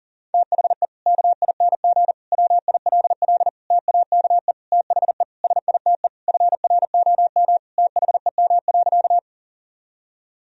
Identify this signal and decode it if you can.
Morse code, heard as a keyed tone, about 40 wpm, 700 Hz. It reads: THE KING WILL TAKE THE SITE FROM THEM.
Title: Sentences from Top 1000 Words - Review Part 4 - 40wpm